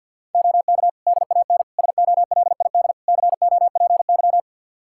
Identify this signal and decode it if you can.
Morse code, heard as a keyed tone, about 51 wpm, 700 Hz. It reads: OK DAN SOLID COPY
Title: Practice QSO - Exchange 3 - More Info - 50wpm